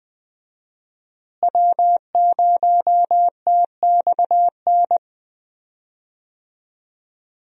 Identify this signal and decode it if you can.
Morse code, heard as a keyed tone, about 20 wpm, 700 Hz. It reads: W0TXN
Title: USA Call Signs - 5 Characters - Repeated 3x - 20wpm